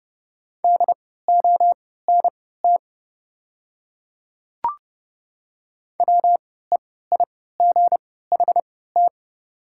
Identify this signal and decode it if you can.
Morse code, heard as a keyed tone, about 32 wpm, spaced as for 17 wpm, 690 Hz. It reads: DONT WEIGHT
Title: 3x Character Spacing - Top 500 Words - 30wpm